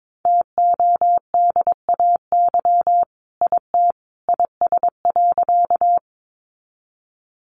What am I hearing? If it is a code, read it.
Morse code, heard as a keyed tone, about 22 wpm, 700 Hz. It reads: TODAY IT IS.
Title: Sentences from Top 600 Words 22wpm